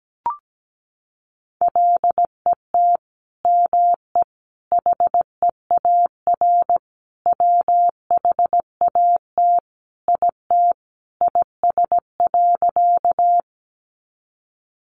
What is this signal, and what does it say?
Morse code, heard as a keyed tone, about 17 wpm, 700 Hz. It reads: LET ME HEAR WHAT IT IS.
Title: Sentences from Top 300 Words 17wpm